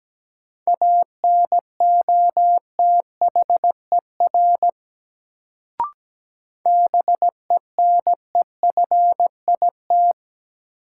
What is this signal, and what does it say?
Morse code, heard as a keyed tone, about 17 wpm, 700 Hz. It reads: ANOTHER BENEFIT
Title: Sets of 5 Words Encore - New Word Drill - 17wpm